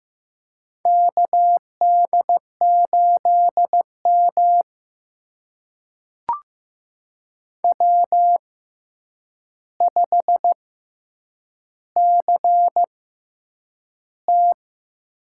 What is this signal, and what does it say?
Morse code, heard as a keyed tone, about 15 wpm, spaced as for 8 wpm, 700 Hz. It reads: KD8M W5CT
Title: USA Call Signs - 4 Characters - 6x Character Spacing - 15wpm